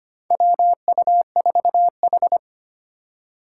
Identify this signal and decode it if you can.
Morse code, heard as a keyed tone, about 25 wpm, 700 Hz. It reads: WU4H
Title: USA Call Signs - 3x Character Spacing - 25wpm